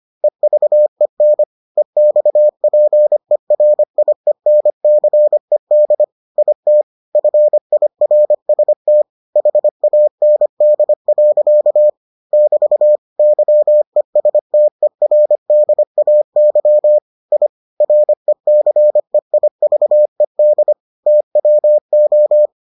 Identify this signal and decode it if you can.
Morse code, heard as a keyed tone, about 25 wpm, 600 Hz. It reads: EVEN EXPERIENCED IT FIRST HAND. <BT> YESTERDAY I RECEIVED TWO